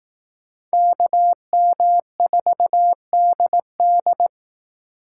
Morse code, heard as a keyed tone, about 19 wpm, 700 Hz. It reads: KM4DD
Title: Speed Racing - USA Call Signs 15wpm